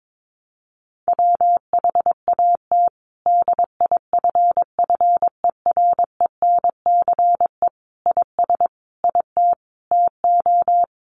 Morse code, heard as a keyed tone, about 22 wpm, 700 Hz. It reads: WHAT DIFFERENCE IS IT TO
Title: Sentences from Top 800 Words 22wpm